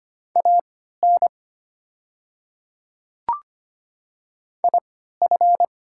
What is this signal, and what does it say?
Morse code, heard as a keyed tone, about 25 wpm, spaced as for 14 wpm, 700 Hz. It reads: AN IF
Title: 3x Character Spacing - Top 100 Words - 25wpm